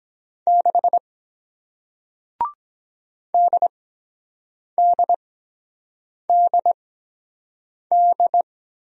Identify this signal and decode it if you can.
Morse code, heard as a keyed tone, about 24 wpm, spaced as for 7 wpm, 700 Hz. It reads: 6 DDDD